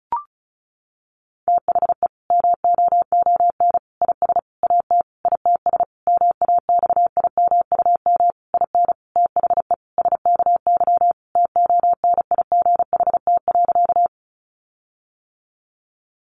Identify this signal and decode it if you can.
Morse code, heard as a keyed tone, about 34 wpm, 700 Hz. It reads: THE MOON IS AT ITS MAXIMUM IN THE SKY TONIGHT.
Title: Sentences from Top 1300 Words 35wpm